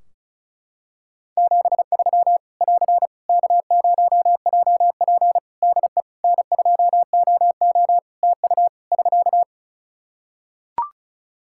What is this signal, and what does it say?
Morse code, heard as a keyed tone, about 36 wpm, 700 Hz. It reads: 73 <AR> K0JP DE N2OO TU <SK>
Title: Practice QSO - Exchange 4 - Ending - 35wpm